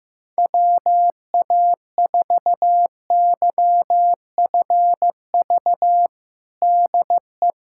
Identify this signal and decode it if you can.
Morse code, heard as a keyed tone, about 15 wpm, 700 Hz. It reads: WA4YFV DE
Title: USA Callsign DE Callsign 15wpm